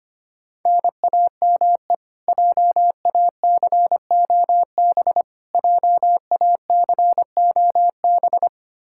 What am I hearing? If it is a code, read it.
Morse code, heard as a keyed tone, about 25 wpm, 700 Hz. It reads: NAME JACOB JACOB
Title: K1USN SST QSO Practice 25wpm